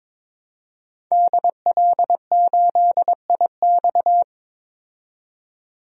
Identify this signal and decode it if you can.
Morse code, heard as a keyed tone, about 22 wpm, 700 Hz. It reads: DL8IX
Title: DXCC-100 Call Signs - 5 Characters - 22wpm